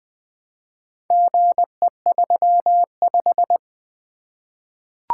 Morse code, heard as a keyed tone, about 20 wpm, 700 Hz. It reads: GE35
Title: Four Letter-Number Combinations - 20wpm